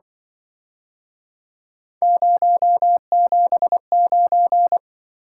Morse code, heard as a keyed tone, about 23 wpm, 700 Hz. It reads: 079
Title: Speed-Racing - Three Letter-Number Combinations - 20wpm